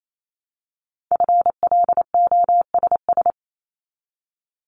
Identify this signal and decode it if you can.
Morse code, heard as a keyed tone, about 28 wpm, 700 Hz. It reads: FLOSS
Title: Multi-letter Phonemes Repeated 3x 28wpm